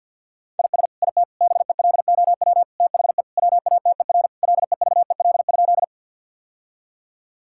Morse code, heard as a keyed tone, about 50 wpm, 700 Hz. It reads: IS IT BELOW THE WATER LEVEL?
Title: Sentences from Top 1000 Words - Review Part 5 - 50wpm